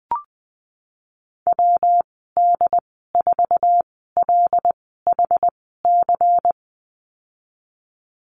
Morse code, heard as a keyed tone, about 20 wpm, spaced as for 15 wpm, 700 Hz. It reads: WD4LHC
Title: 147 Lesson 24g - Letter y - Call Signs 20wpm